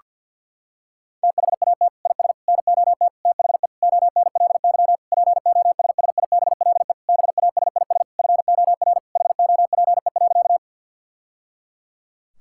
Morse code, heard as a keyed tone, about 50 wpm, 700 Hz. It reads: THAT IS NOT THE ONLY POSSIBLE BASIS FOR HOPE.
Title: Sentences from Top 1000 Words - Review Part 5 - 50wpm